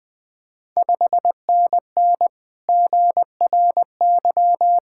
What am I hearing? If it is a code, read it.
Morse code, heard as a keyed tone, about 20 wpm, 690 Hz. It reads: 5NN GRY
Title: Ontario QSO Party Exchange 20wpm